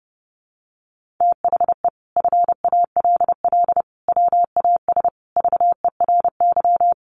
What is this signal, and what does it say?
Morse code, heard as a keyed tone, about 30 wpm, 700 Hz. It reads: THE FALL WAS VERY
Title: Sentences from Top 1400 Words 30wpm